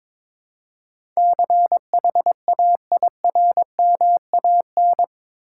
Morse code, heard as a keyed tone, about 21 wpm, 700 Hz. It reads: CHAIRMAN
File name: practice / 22wpm / Eight Letter Words - 22wpm